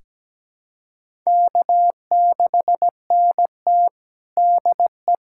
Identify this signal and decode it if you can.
Morse code, heard as a keyed tone, about 17 wpm, 700 Hz. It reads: K6NT DE
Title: USA Callsign DE Callsign - 4 Characters - 17wpm